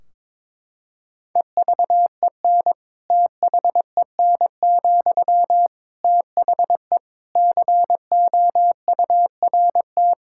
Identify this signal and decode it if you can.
Morse code, heard as a keyed tone, about 22 wpm, 700 Hz. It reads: EVEN THEN, THE COURT
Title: Sentences from Top 1600 Words 22wpm